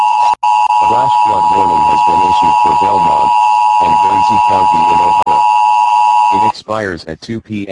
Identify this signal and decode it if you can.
Emergency Alert System (SAME) tones (the 853 and 960 Hz attention tone), with speech. It is heard from 0.4 to 6.5 s.